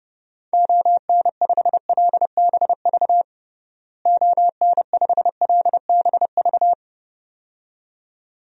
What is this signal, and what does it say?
Morse code, heard as a keyed tone, about 30 wpm, 700 Hz. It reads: ON5LBV ON5LBV
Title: DXCC-100 Call Signs - 6 Characters - Repeated 2x - 30wpm